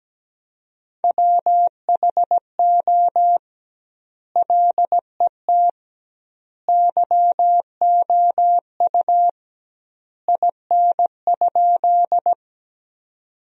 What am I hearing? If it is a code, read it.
Morse code, heard as a keyed tone, about 17 wpm, 700 Hz. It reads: WHO LET YOU IN?
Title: 2x Word-Spacing - Sentences from Top 300 Words - 17wpm